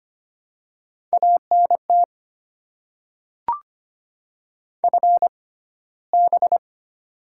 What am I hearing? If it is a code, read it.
Morse code, heard as a keyed tone, about 25 wpm, spaced as for 15 wpm, 700 Hz. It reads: ANT FB
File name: Top 20 QSO Elements - 6x Character Spacing - 25wpm